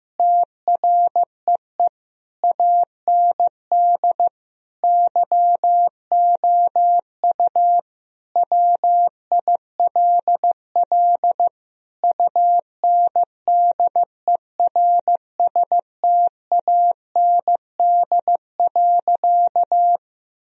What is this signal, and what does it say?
Morse code, heard as a keyed tone, about 15 wpm, 700 Hz. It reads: TREE AND YOU WILL UNDERSTAND.
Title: Sentences from Top 1000 Words - Review Part 1 - 15wpm